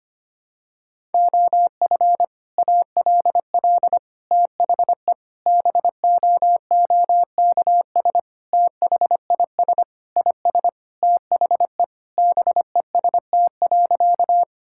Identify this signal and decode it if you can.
Morse code, heard as a keyed tone, about 25 wpm, 700 Hz. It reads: OF ALL THE BOOKS THIS IS THE BEST.
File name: practice / 25wpm / Sentences with 3-Word Phrases - 25wpm